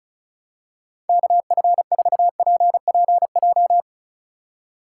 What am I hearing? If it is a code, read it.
Morse code, heard as a keyed tone, about 36 wpm, 700 Hz. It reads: KF4PPJ
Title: USA Call Signs - 6 Characters - 3x Character Spacing - 35wpm